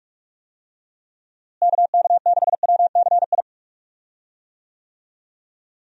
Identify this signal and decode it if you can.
Morse code, heard as a keyed tone, about 46 wpm, 700 Hz. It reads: KK6WCI
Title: USA Call Signs - 6 Characters - Repeated 2x - 45wpm